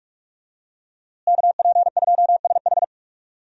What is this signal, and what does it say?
Morse code, heard as a keyed tone, about 46 wpm, 700 Hz. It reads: KP2SH
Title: DXCC-100 Call Signs - 5 Characters - Repeated 2x - 45wpm